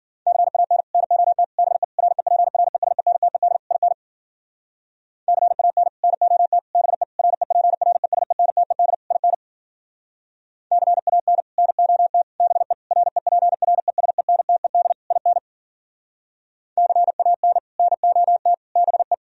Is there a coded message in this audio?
Yes: Morse code from 0.3 to 19.1 s, heard as a keyed tone, about 52 wpm, 700 Hz. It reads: CAN NOT BE REPRESENTED IN CAN NOT BE REPRESENTED IN CAN NOT BE REPRESENTED IN CAN NOT BE